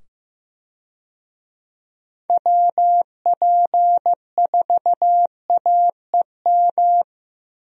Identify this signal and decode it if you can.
Morse code, heard as a keyed tone, about 15 wpm, 700 Hz. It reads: WP4AEM
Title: USA Call Signs - 6 Characters - 15wpm